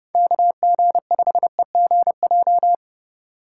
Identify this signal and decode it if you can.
Morse code, heard as a keyed tone, about 32 wpm, 700 Hz. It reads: KG5EGJ
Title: USA Call Signs - 2x Character Spacing - 30wpm